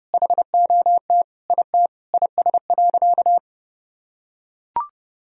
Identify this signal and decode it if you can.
Morse code, heard as a keyed tone, about 30 wpm, 700 Hz. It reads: HOT IT IS.
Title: Sentences from Top 1000 Words - Review Part 3 - 30wpm